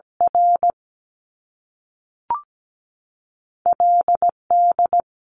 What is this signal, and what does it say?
Morse code, heard as a keyed tone, about 17 wpm, 700 Hz. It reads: R LD
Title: Top Two Letter Combinations in Words 17wpm